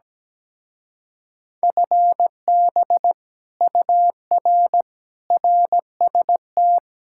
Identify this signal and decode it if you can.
Morse code, heard as a keyed tone, about 17 wpm, 700 Hz. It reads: FB UR RST
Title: Speed-Racing Common QSO Phrases - 17wpm